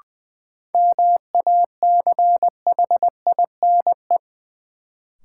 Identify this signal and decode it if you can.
Morse code, heard as a keyed tone, about 20 wpm, 700 Hz. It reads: MACHINE
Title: Sentences from Top 1000 Words - Review Part 2 - New Word Drill - 20wpm